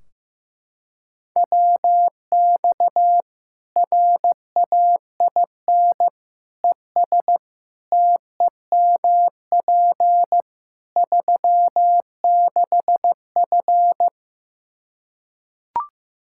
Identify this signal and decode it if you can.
Morse code, heard as a keyed tone, about 15 wpm, 700 Hz. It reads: WX RAIN ES TEMP 36F
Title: Practice QSO - Exchange 2 - Info - 15wpm